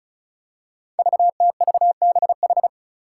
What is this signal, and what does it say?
Morse code, heard as a keyed tone, about 34 wpm, 700 Hz. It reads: VTVBH
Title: Five Challenging Letter-Number Combinations 35wpm